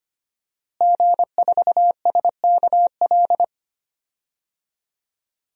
Morse code, heard as a keyed tone, about 25 wpm, 700 Hz. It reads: G4SKL
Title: DXCC-100 Call Signs - 5 Characters - Repeated 3x - 25wpm